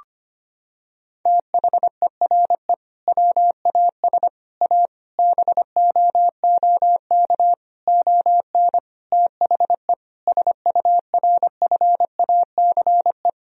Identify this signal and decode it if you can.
Morse code, heard as a keyed tone, about 25 wpm, 700 Hz. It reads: THERE WAS A BOOK ON THE SURFACE
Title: Sentences from Top 1000 Words - Review Part 3 - 25wpm